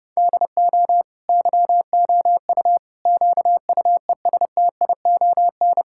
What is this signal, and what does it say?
Morse code, heard as a keyed tone, about 32 wpm, 700 Hz. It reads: DO YOU QUESTION
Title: Sentences from Top 1000 Words - Review Part 5 - 30wpm